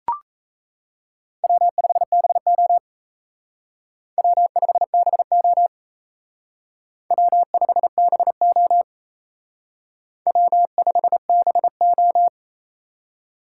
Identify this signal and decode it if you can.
Morse code, heard as a keyed tone, about 35 wpm, 700 Hz. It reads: W5BO W5BO W5BO W5BO
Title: Speed Racing - USA 1x2 Call Signs 28wpm